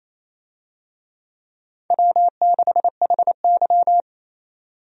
Morse code, heard as a keyed tone, about 27 wpm, 700 Hz. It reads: W6HY
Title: USA Call Signs - 4 Characters - 28wpm